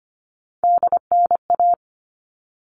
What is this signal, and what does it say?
Morse code, heard as a keyed tone, about 25 wpm, 700 Hz. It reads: DNA
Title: All Cut Numbers - 3-Digit 25wpm